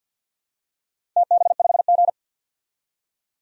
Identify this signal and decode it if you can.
Morse code, heard as a keyed tone, about 52 wpm, 700 Hz. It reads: TB5G
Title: DXCC-100 Call Signs - 4 Characters - 50wpm